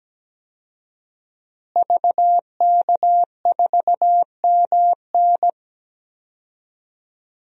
Morse code, heard as a keyed tone, about 17 wpm, 700 Hz. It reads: VK4MN
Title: DXCC-100 Call Signs - 5 Characters - Repeated 2x - 17wpm